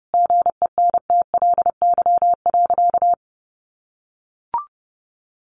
Morse code, heard as a keyed tone, about 28 wpm, 700 Hz. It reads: GENTLY.
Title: Sentences from Top 1500 Words 30wpm